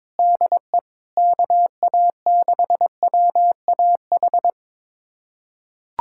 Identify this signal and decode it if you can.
Morse code, heard as a keyed tone, about 22 wpm, 700 Hz. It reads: DE KA6WAH
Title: USA Callsign DE Callsign - 6 Characters - 22wpm